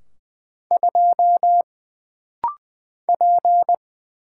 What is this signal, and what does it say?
Morse code, heard as a keyed tone, about 20 wpm, 700 Hz. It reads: 2 P
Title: Single Letter-Number - Rapid-Fire 20wpm